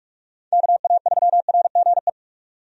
Morse code, heard as a keyed tone, about 46 wpm, 700 Hz. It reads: KA3RGE
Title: USA Call Signs - 6 Characters - 45wpm